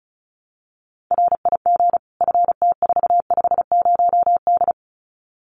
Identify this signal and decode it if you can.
Morse code, heard as a keyed tone, about 34 wpm, 700 Hz. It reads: RIG FT450D
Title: Common QSO Phrases 35wpm